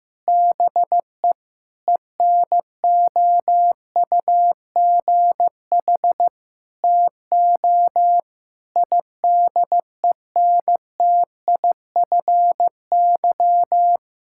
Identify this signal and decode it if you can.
Morse code, heard as a keyed tone, about 15 wpm, 700 Hz. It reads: BE ENOUGH TO IDENTIFY
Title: Sentences from Top 1000 Words - Review Part 6 - 15wpm